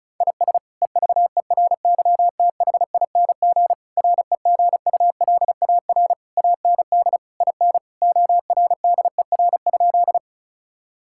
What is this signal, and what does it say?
Morse code, heard as a keyed tone, about 36 wpm, 700 Hz. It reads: IS EVERYTHING REGULAR AND IN ORDER?
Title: Sentences from Top 1000 Words - Review Part 6 - 35wpm